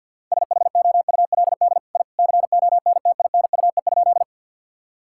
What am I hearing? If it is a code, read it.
Morse code, heard as a keyed tone, about 49 wpm, 700 Hz. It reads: SHOULD I CONTINUE?
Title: Sentences from Top 1000 Words - Review Part 6 - 50wpm